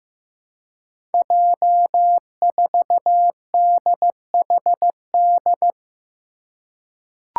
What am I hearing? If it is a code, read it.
Morse code, heard as a keyed tone, about 15 wpm, 700 Hz. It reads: J4DHD
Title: Five Challenging Letter-Number Combinations 15wpm